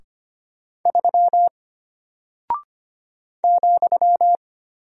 Morse code, heard as a keyed tone, about 25 wpm, 700 Hz. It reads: 3 ,
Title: Single Character (Letter:Number:Punctuation:Prosign) Alt 25wpm